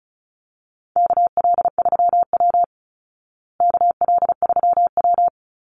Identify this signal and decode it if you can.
Morse code, heard as a keyed tone, about 34 wpm, 700 Hz. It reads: KL3W KL3W
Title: Call Signs Repeated 2x - 35wpm